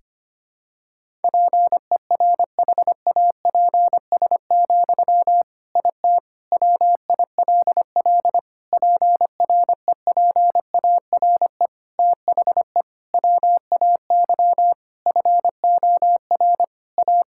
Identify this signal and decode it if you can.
Morse code, heard as a keyed tone, about 25 wpm, 700 Hz. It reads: PERHAPS, IT WILL PREPARE THE WAY FOR A
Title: Sentences from Top 1000 Words - Review Part 6 - 25wpm